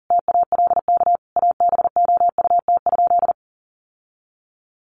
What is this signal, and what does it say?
Morse code, heard as a keyed tone, about 40 wpm, 700 Hz. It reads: TALK ABOUT?
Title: Sentences from Top 1100 Words 40wpm